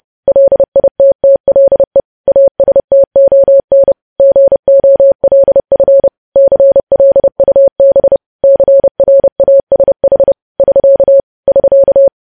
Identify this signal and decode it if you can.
Morse code, heard as a keyed tone, about 30 wpm, 550 Hz. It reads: LITTLE ASTON GOLF CLUB CRASH <SK> <SK>
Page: News Headlines in Morse Code at 30 WPM